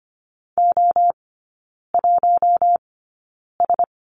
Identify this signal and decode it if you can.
Morse code, heard as a keyed tone, about 25 wpm, spaced as for 9 wpm, 700 Hz. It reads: O1S
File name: course / 25:10wpm / 093 Lesson 16e - Letter c - Three Characters 25wpm